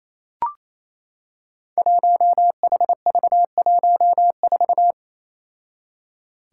Morse code, heard as a keyed tone, about 28 wpm, 700 Hz. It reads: 1HV14
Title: Five Challenging Letter-Number Combinations 28wpm